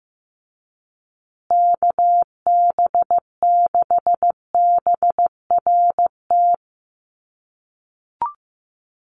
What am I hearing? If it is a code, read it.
Morse code, heard as a keyed tone, about 15 wpm, 700 Hz. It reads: KB6BRT
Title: USA Call Signs Repeated 3x 15wpm